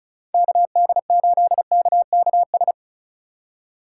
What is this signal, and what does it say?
Morse code, heard as a keyed tone, about 36 wpm, 700 Hz. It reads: KD8KKS
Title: USA Call Signs - 6 Characters - 3x Character Spacing - 35wpm